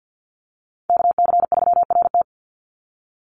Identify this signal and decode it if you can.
Morse code, heard as a keyed tone, about 50 wpm, 700 Hz. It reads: KC3RT
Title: NA SOTA Chaser Callsigns 50wpm